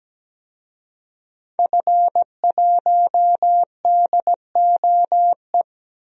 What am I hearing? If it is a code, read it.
Morse code, heard as a keyed tone, about 17 wpm, 700 Hz. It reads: F1DOE
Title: DXCC-100 Call Signs - 5 Characters - Repeated 2x - 17wpm